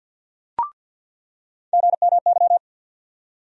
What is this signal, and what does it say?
Morse code, heard as a keyed tone, about 50 wpm, 700 Hz. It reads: GMY
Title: USA Call Sign Postfixes - 3 Letters 50wpm